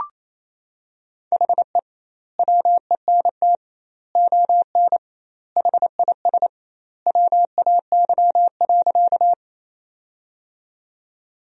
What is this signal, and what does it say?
Morse code, heard as a keyed tone, about 28 wpm, 700 Hz. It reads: HE WENT ON HIS WAY.
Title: 2x Word-Spacing - Sentences from Top 200 Words - 28wpm